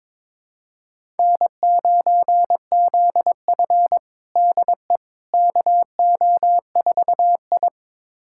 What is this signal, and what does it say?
Morse code, heard as a keyed tone, about 22 wpm, 700 Hz. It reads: N9ZF DE KO4I
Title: USA Callsign DE Callsign - 4 Characters - 22wpm